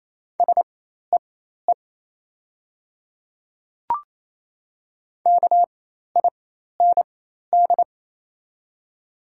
Morse code, heard as a keyed tone, about 28 wpm, spaced as for 13 wpm, 700 Hz. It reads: SEE KIND
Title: 4x Character Spacing - Top 500 Words - 28wpm